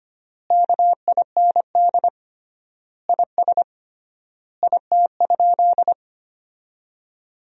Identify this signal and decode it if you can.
Morse code, heard as a keyed tone, about 25 wpm, 700 Hz. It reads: KIND IS IT?
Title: 3x Word-Spacing - Sentences from Top 200 Words - 25wpm